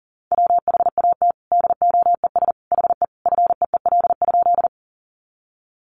Morse code, heard as a keyed tone, about 37 wpm, 700 Hz. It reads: WHAT DOES HE FEEL?
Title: Sentences from Top 400 Words 40wpm